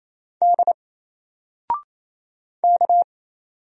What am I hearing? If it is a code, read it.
Morse code, heard as a keyed tone, about 27 wpm, 700 Hz. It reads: D K
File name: Single-Letter-Number-28wpm